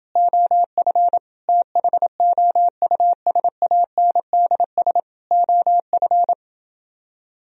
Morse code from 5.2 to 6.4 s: OF

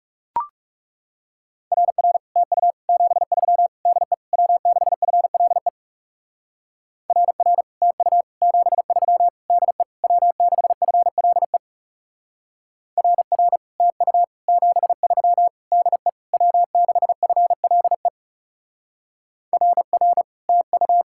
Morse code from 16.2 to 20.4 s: W6FLE RR

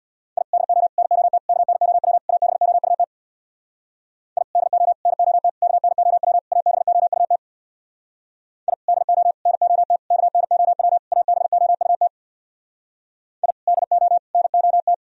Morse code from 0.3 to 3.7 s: I DO NOT KNOW ABOUT

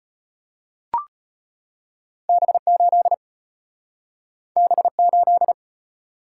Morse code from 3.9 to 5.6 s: B8

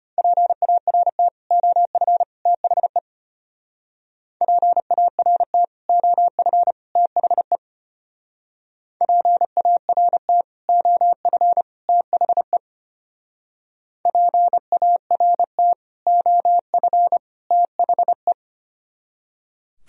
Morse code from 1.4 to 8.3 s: OF THE PART OF THE

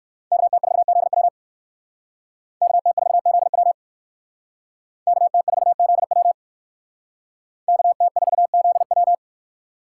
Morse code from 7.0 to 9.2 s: KT4ZW

